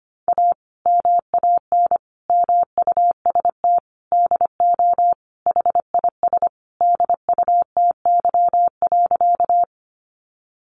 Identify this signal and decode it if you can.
Morse code, heard as a keyed tone, about 25 wpm, 700 Hz. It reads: A MAN MUST DO HIS DUTY.
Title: Sentences from Top 1500 Words 25wpm